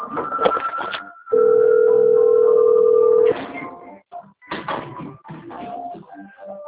Ringback tone (North America), 440 + 480 Hz.